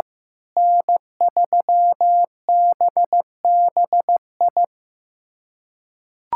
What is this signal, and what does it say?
Morse code, heard as a keyed tone, about 15 wpm, 700 Hz. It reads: N3BBI
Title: USA Callsign DE Callsign - 5 Characters - 15wpm